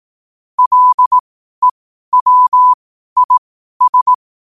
Morse code, heard as a keyed tone, about 18 wpm, spaced as for 12 wpm, 1000 Hz. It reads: LEWIS
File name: qotdfiles / 12wpm